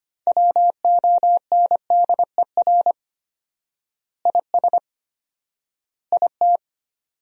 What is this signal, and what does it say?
Morse code, heard as a keyed tone, about 25 wpm, 700 Hz. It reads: WONDER IS IT
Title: 4x Word-Spacing - Sentences from Top 500 Words - 25wpm